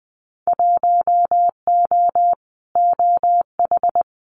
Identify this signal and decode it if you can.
Morse code, heard as a keyed tone, about 20 wpm, 700 Hz. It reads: 1O OH
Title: Winter Field Day Exchanges 20wpm